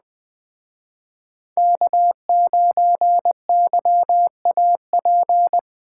Morse code, heard as a keyed tone, about 21 wpm, 700 Hz. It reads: K9YAP
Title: Speed Racing - USA 1x3 Call Signs 17wpm